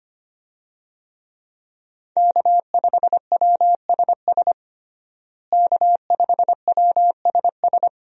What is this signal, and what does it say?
Morse code, heard as a keyed tone, about 25 wpm, 700 Hz. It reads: K5WSS K5WSS
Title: USA Call Signs - 5 Characters - Repeated 2x - 25wpm